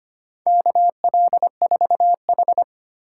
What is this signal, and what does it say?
Morse code, heard as a keyed tone, about 25 wpm, 700 Hz. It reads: KL4H
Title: USA Call Signs - 4 Characters - 4x Character Spacing - 25wpm